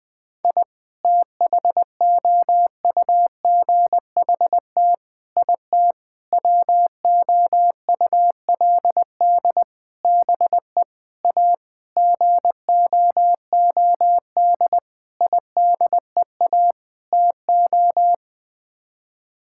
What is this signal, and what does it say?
Morse code, heard as a keyed tone, about 20 wpm, 700 Hz. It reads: I THOUGHT IT WOULD BE A GOOD IDEA TO
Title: Sets of 9 Words - 20wpm